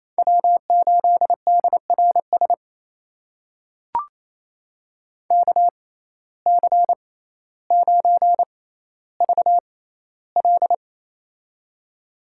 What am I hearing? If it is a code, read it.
Morse code, heard as a keyed tone, about 27 wpm, spaced as for 14 wpm, 700 Hz. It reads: W8DRS KC9VL
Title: USA Call Signs - 5 Characters - 6x Character Spacing - 28wpm